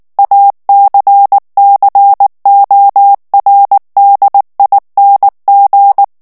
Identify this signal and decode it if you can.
Morse code, heard as a keyed tone, about 19 wpm, 800 Hz. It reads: ACCORDING
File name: ANGLAIS texte clair 2 vitesse 18 mots minute